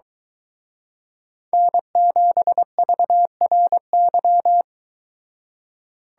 Morse code, heard as a keyed tone, about 23 wpm, 700 Hz. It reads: N7VRY